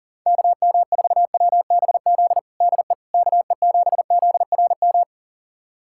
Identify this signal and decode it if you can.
Morse code, heard as a keyed tone, about 37 wpm, 700 Hz. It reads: KM4WBZ DE KE7ZRM
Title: USA Callsign DE Callsign 40wpm